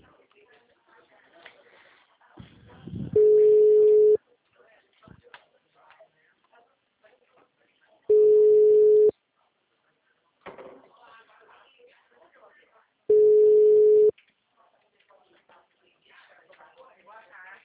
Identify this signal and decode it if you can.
Ringback tone (Europe, 425 Hz), with speech.
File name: out- -2033- - - .66759